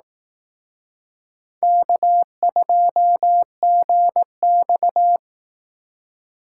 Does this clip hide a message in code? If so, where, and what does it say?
Yes: Morse code from 1.6 to 5.2 s, heard as a keyed tone, about 19 wpm, 700 Hz. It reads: K2GX